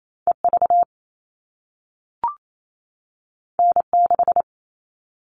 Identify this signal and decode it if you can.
Morse code, heard as a keyed tone, about 28 wpm, 700 Hz. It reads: EV N6
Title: All Cut Numbers - 2-Digit 28wpm